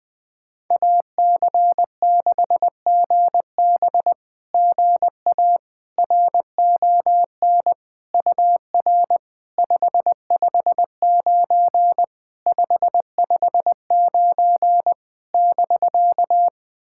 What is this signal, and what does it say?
Morse code, heard as a keyed tone, about 20 wpm, 700 Hz. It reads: AC6GB GA RON UR 559 559 <BK>
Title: SOTA QSO Practice 20wpm